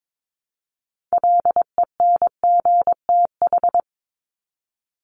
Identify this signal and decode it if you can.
Morse code, heard as a keyed tone, about 22 wpm, 700 Hz. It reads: LENGTH
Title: Top 751-1000 Words 22wpm